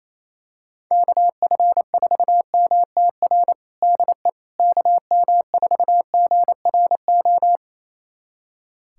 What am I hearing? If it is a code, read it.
Morse code, heard as a keyed tone, about 28 wpm, 700 Hz. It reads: KF4MTR DE KM4GRO
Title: USA Callsign DE Callsign 28wpm